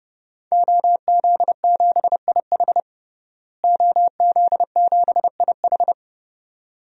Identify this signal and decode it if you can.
Morse code, heard as a keyed tone, about 32 wpm, 700 Hz. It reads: OZ7IH OZ7IH
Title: DXCC-100 Call Signs - 5 Characters - Repeated 2x - 30wpm